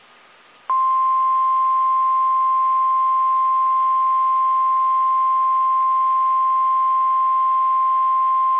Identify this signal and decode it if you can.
Emergency Alert System (SAME) tones (the 1050 Hz attention tone). It is heard from 0.8 to 7.7 s.
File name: eas / KSDZKDJL / KSDZKDJL